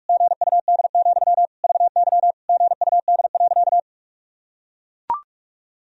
Morse code, heard as a keyed tone, about 42 wpm, 700 Hz. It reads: GUD, VY GUD.
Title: Sentences from Top 200 Words and Abbreviations 45wpm